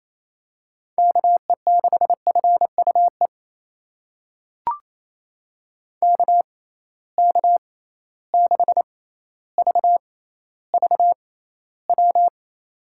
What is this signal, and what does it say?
Morse code, heard as a keyed tone, about 28 wpm, spaced as for 14 wpm, 700 Hz. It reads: KE6FUE KK6VVW
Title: USA Call Signs - 6 Characters - 6x Character Spacing - 28wpm